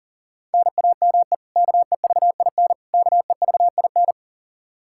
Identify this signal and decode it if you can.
Morse code, heard as a keyed tone, about 40 wpm, 700 Hz. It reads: NAME KEVIN KEVIN
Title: Practice QSO - Exchange 1 - Beginning - 40wpm